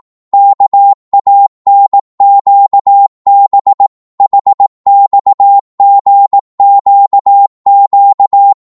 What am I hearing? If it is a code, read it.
Morse code, heard as a keyed tone, about 18 wpm, 800 Hz. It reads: KANQB HXGQQ
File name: Morse / ltr-18wpm